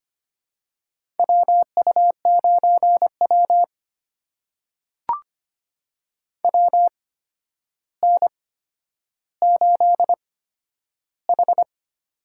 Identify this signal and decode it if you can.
Morse code, heard as a keyed tone, about 25 wpm, spaced as for 11 wpm, 700 Hz. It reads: WU9W WN8H
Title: USA Call Signs - 4 Characters - 8x Character Spacing - 25wpm